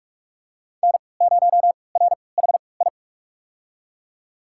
Morse code, heard as a keyed tone, about 46 wpm, spaced as for 26 wpm, 700 Hz. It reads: N0RHI
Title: USA Call Signs - 3x Character Spacing - 45wpm